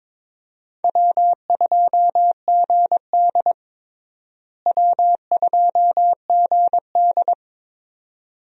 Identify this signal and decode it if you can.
Morse code, heard as a keyed tone, about 22 wpm, 700 Hz. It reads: W2GD W2GD
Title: USA Call Signs - 4 Characters - Repeated 2x - 22wpm